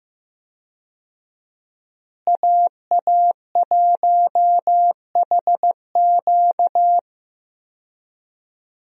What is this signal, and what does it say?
Morse code, heard as a keyed tone, about 15 wpm, 700 Hz. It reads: AA1HQ
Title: USA Call Signs - 5 Characters - Repeated 2x - 15wpm